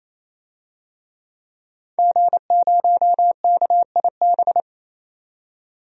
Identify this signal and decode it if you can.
Morse code, heard as a keyed tone, about 28 wpm, 700 Hz. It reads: G0KIB
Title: DXCC-100 Call Signs - 5 Characters - 28wpm